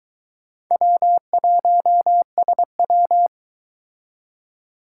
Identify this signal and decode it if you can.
Morse code, heard as a keyed tone, about 23 wpm, 700 Hz. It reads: W1SW